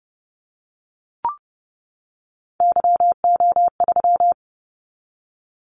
Morse code, heard as a keyed tone, about 30 wpm, 700 Hz. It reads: YO3
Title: Callsign Prefixes for Top 25 DXCC - 30wpm